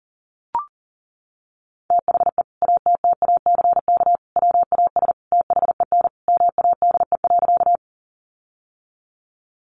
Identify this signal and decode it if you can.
Morse code, heard as a keyed tone, about 40 wpm, 700 Hz. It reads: THE ATTACK WAS THEN MADE.
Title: Sentences from Top 1500 Words 40wpm